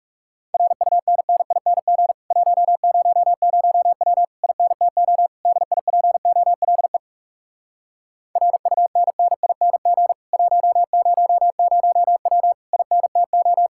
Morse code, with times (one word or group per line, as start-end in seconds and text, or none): 0.5-2.1 RUNNING
2.3-4.2 100W
4.4-5.3 INTO
5.5-7.0 DIPOLE
8.4-10.1 RUNNING
10.3-12.5 100W
12.7-13.7 INTO